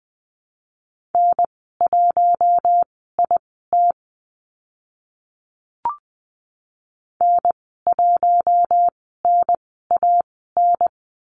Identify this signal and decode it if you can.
Morse code, heard as a keyed tone, about 20 wpm, spaced as for 15 wpm, 700 Hz. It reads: N1IT N1NAN
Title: 042 Lesson 8g - Number 1 - Call Signs 20wpm